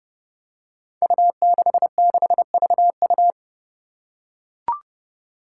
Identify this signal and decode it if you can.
Morse code, heard as a keyed tone, about 32 wpm, 700 Hz. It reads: U66VU
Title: Five Challenging Letter-Number Combinations 30wpm